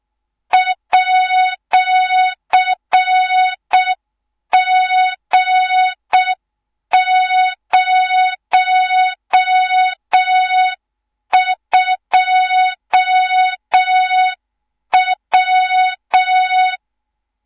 Morse code, heard as a keyed tone, about 6 wpm, 3700 Hz, with noise. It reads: @G02W
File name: Morze-HA2MN / hang / 24 GYAKORLAT